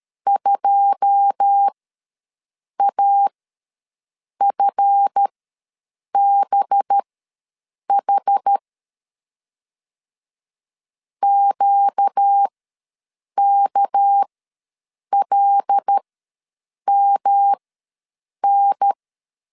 Morse code, heard as a keyed tone, about 13 wpm, spaced as for 7 wpm, 800 Hz. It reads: 2AFBH QKLMN
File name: LEZIONI / 9 2867 ws 12wpm 3431